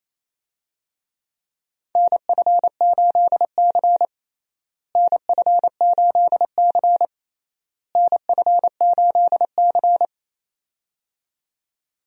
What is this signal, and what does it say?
Morse code, heard as a keyed tone, about 28 wpm, 700 Hz. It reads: NF8C NF8C NF8C
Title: USA Call Signs - 4 Characters - Repeated 3x - 28wpm